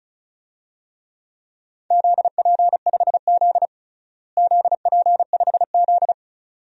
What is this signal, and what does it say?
Morse code, heard as a keyed tone, about 34 wpm, 690 Hz. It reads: ZP5Z ZP5Z
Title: DXCC-100 Call Signs - 4 Characters - Repeated 2x - 35wpm